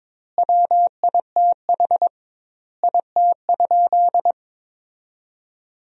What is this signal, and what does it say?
Morse code, heard as a keyed tone, about 22 wpm, 700 Hz. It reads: WITH IT?
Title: 2x Word-Spacing - Sentences from Top 100 Words - 22wpm